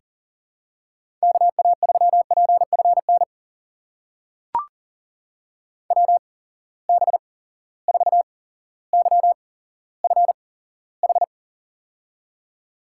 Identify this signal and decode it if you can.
Morse code, heard as a keyed tone, about 40 wpm, spaced as for 17 wpm, 700 Hz. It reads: KA3PFN WB4YFH
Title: USA Call Signs - 6 Characters - 8x Character Spacing - 40wpm